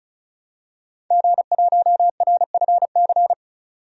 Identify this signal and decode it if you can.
Morse code, heard as a keyed tone, about 34 wpm, 700 Hz. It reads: G1RFC
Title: DXCC-100 Call Signs - 5 Characters - 35wpm